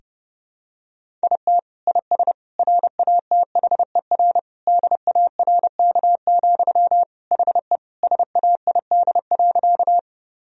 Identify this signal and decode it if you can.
Morse code, heard as a keyed tone, about 31 wpm, 700 Hz. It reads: IT IS RATHER DARK, HE SAID.
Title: Sentences from Top 1000 Words - Review Part 4 - 30wpm